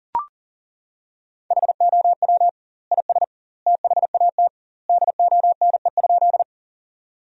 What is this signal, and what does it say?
Morse code, heard as a keyed tone, about 37 wpm, 700 Hz. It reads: HOW IS THAT DONE?
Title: 2x Word-Spacing - Sentences from Top 500 Words - 40wpm